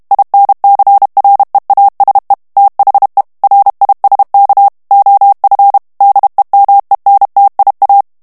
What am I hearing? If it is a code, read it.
Morse code, heard as a keyed tone, about 32 wpm, 800 Hz. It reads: INCREASE THE RISK OF DEMENTIA
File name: morse mp3 / ANGLAIS texte clair 1 vitesse 30 mots minute